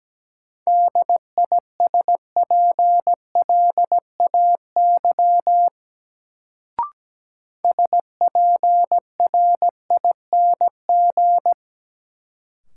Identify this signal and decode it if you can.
Morse code, heard as a keyed tone, about 17 wpm, 700 Hz. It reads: DISPLAY SPRING
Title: Sentences from Top 1200 Words - New Word Drill Alt - 17wpm